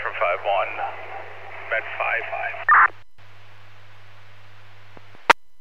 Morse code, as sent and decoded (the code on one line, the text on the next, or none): none